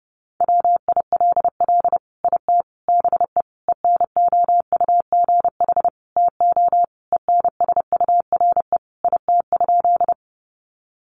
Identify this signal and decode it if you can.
Morse code, heard as a keyed tone, about 30 wpm, 700 Hz. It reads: WILL IT BE ENOUGH TO ENSURE IT?
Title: Sentences from Top 1100 Words 30wpm